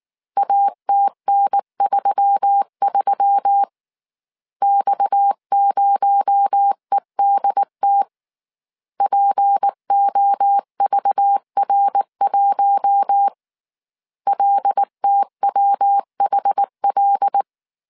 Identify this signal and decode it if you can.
Morse code, heard as a keyed tone, about 19 wpm, 800 Hz. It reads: ATN33 X0EDT POVR1 LTWHL
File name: CW / DET / 1 6 1500C S C 18wpm 2356